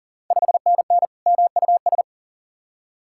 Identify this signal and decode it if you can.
Morse code, heard as a keyed tone, about 40 wpm, 690 Hz. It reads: 5NN MUS
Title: Ontario QSO Party Exchange 40wpm